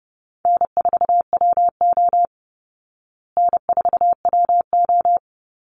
Morse code, heard as a keyed tone, about 30 wpm, 700 Hz. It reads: N4WO N4WO
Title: Call Signs Repeated 3x - 30wpm